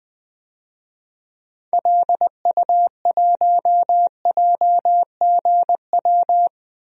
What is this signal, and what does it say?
Morse code, heard as a keyed tone, about 21 wpm, 700 Hz. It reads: LU1JGW
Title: DXCC-100 Call Signs - 6 Characters - 20wpm